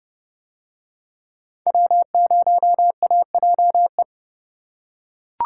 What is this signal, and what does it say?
Morse code, heard as a keyed tone, about 29 wpm, 700 Hz. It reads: W0AJE